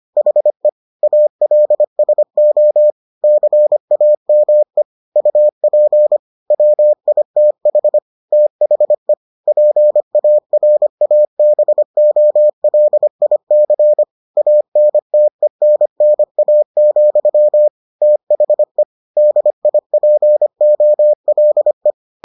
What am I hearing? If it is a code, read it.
Morse code, heard as a keyed tone, about 25 wpm, 600 Hz. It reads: HE ALSO CAME UP WITH THE PARABOLIC ANTENNA, THE DIPOLE